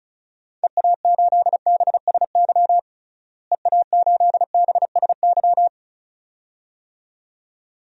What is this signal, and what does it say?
Morse code, heard as a keyed tone, about 34 wpm, 700 Hz. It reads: EA8BSY EA8BSY
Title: DXCC-100 Call Signs - 6 Characters - Repeated 2x - 35wpm